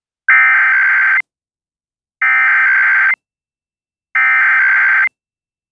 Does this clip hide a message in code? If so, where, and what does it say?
Yes: Emergency Alert System (SAME) tones (header bursts) from 0.3 to 5.0 s, heard as harsh buzzing bursts.